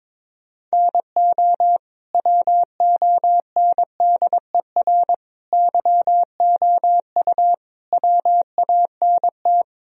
Morse code, heard as a keyed tone, about 22 wpm, 700 Hz. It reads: NO WONDER YOU WANT
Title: Sentences from Top 1000 Words - Review Part 3 - 22wpm